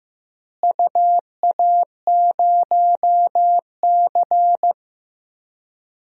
Morse code, heard as a keyed tone, about 15 wpm, 700 Hz. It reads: UA0C
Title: DXCC-100 Call Signs - 4 Characters - Repeated 2x - 15wpm